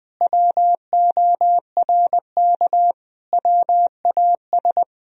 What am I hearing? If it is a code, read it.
Morse code, heard as a keyed tone, about 20 wpm, 700 Hz. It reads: WORK WAS
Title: Sentences from Top 1000 Words - Review Part 1 - 20wpm